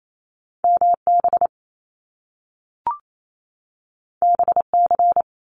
Morse code, heard as a keyed tone, about 28 wpm, 700 Hz. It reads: MB BC
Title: Canadian Provinces and Territories 28wpm